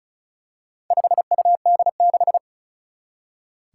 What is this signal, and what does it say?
Morse code, heard as a keyed tone, about 36 wpm, 700 Hz. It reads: 5UD6
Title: Four Challenging Letter-Number Combinations 35wpm